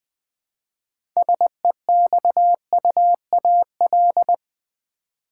Morse code, heard as a keyed tone, about 20 wpm, 700 Hz. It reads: SEXUAL
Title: Six Letter Words - 20wpm